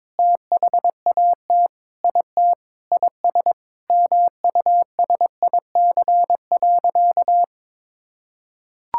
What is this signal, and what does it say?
Morse code, heard as a keyed tone, about 22 wpm, 700 Hz. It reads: THAT IT IS MUSIC.
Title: Sentences from Top 1600 Words 22wpm